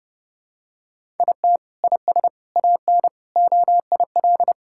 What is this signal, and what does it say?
Morse code, heard as a keyed tone, about 30 wpm, 700 Hz. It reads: IT IS AN OIL
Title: Sentences from Top 1000 Words - Review Part 6 - 30wpm